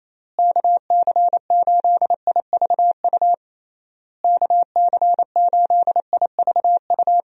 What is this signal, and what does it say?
Morse code, heard as a keyed tone, about 27 wpm, 700 Hz. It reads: KC8IVU KC8IVU
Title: USA Call Signs - 6 Characters - Repeated 3x - 28wpm